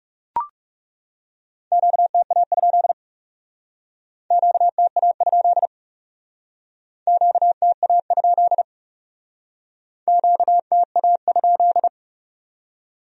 Morse code, heard as a keyed tone, about 36 wpm, 700 Hz. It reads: QTA? QTA? QTA? QTA?
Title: Speed-Racing Q-Codes - 30wpm